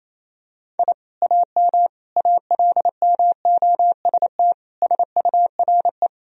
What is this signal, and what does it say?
Morse code, heard as a keyed tone, about 28 wpm, 700 Hz. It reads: I AM ALMOST SURE
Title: Sentences from Top 1000 Words - Review Part 4 - 28wpm